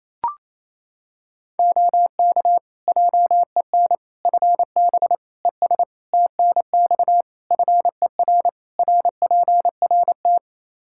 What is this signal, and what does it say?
Morse code, heard as a keyed tone, about 28 wpm, 700 Hz. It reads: OK JEN FB ES TNX FER RPRT
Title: Practice QSO - Exchange 2 - Info - 28wpm